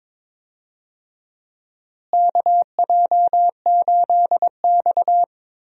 Morse code, heard as a keyed tone, about 22 wpm, 700 Hz. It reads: KJ8X
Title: USA Call Signs - 4 Characters - Repeated 3x - 22wpm